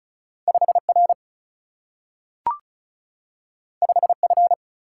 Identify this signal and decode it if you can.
Morse code, heard as a keyed tone, about 36 wpm, 700 Hz. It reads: 5 R 5 F
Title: Single-Letter Single-Number 35wpm